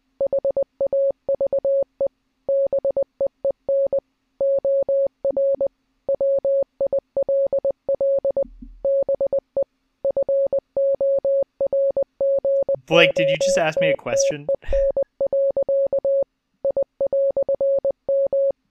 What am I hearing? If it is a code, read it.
Morse code, heard as a keyed tone, about 20 wpm, 550 Hz, with speech. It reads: HAVE BEEN OR WILL BE FORGOTTEN. I"M